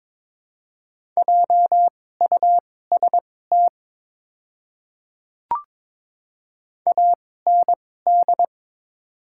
Morse code, heard as a keyed tone, about 22 wpm, spaced as for 16 wpm, 700 Hz. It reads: JUST AND
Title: 2x Character Spacing - Top 200 Words - 22wpm